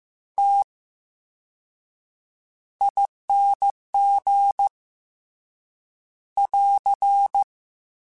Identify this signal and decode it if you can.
Morse code, heard as a keyed tone, about 15 wpm, spaced as for 7 wpm, 800 Hz. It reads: TING <AR>
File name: Special INGs-mp3s / Special INGs-mp3s / VerySlowToFaster Ing @13